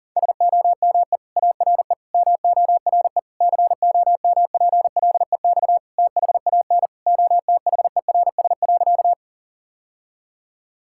Morse code, heard as a keyed tone, about 40 wpm, 700 Hz. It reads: SOME ARE MORE COMPLEX THAN OTHERS.
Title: Sentences from Top 1000 Words - Review Part 6 - 40wpm